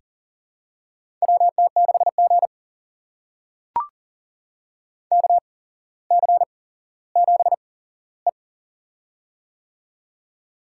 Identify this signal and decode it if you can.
Morse code, heard as a keyed tone, about 40 wpm, spaced as for 17 wpm, 700 Hz. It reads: WT6G KC7E
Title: USA Call Signs - 4 Characters - 8x Character Spacing - 40wpm